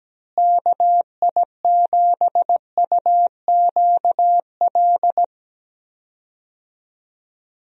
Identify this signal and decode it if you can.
Morse code, heard as a keyed tone, about 17 wpm, 700 Hz. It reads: KI7UQL
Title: USA Call Signs - 6 Characters - Repeated 2x - 17wpm